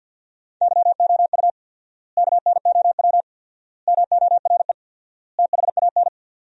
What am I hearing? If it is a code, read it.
Morse code, heard as a keyed tone, about 50 wpm, 700 Hz. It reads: YOU KNOW MORE THAN
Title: 4x Word-Spacing - Sentences from Top 100 Words - 50wpm